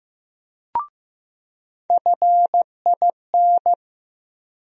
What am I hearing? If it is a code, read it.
Morse code, heard as a keyed tone, about 15 wpm, 700 Hz. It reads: FIN
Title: USA Call Sign Postfixes - 3 Letters 15wpm